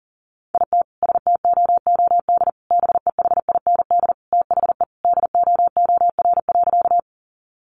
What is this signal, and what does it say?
Morse code, heard as a keyed tone, about 40 wpm, 700 Hz. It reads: IT STOOD BEHIND THE DOOR.
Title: Sentences from Top 500 Words 40wpm